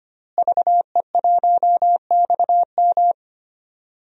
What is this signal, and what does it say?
Morse code, heard as a keyed tone, about 25 wpm, 700 Hz. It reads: VE1XM
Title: DXCC-100 Call Signs - 5 Characters - Repeated 2x - 25wpm